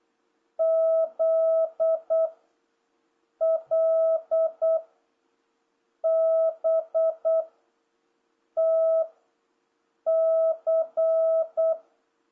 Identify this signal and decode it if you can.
Morse code, heard as a keyed tone, about 8 wpm, 650 Hz. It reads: Z L B T C